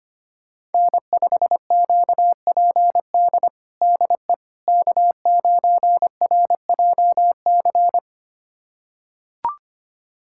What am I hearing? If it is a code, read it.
Morse code, heard as a keyed tone, about 25 wpm, 700 Hz. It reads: N5QPD DE K9RJC
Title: USA Callsign DE Callsign - 5 Characters - 25wpm